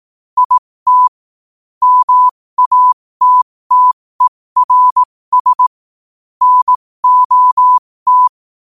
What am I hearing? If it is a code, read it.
Morse code, heard as a keyed tone, about 18 wpm, spaced as for 15 wpm, 1000 Hz. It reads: IT MATTERS NOT
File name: qotdfiles / 15wpm